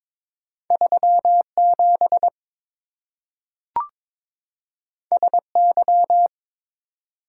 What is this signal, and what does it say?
Morse code, heard as a keyed tone, about 23 wpm, 700 Hz. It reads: 37 SY
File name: practice / 22wpm / Two Letter-Number Combinations - 22wpm